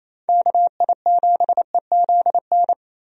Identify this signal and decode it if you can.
Morse code, heard as a keyed tone, about 28 wpm, 700 Hz. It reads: KI7EZN
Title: USA Call Signs - 6 Characters - Repeated 2x - 28wpm